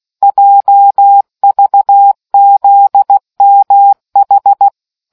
Morse code, heard as a keyed tone, about 16 wpm, 800 Hz. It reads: JVZMH